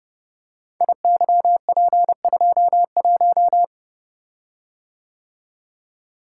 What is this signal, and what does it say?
Morse code, heard as a keyed tone, about 28 wpm, 700 Hz. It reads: IYP21
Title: Code Groups - 5 Letter:Numbers - 30wpm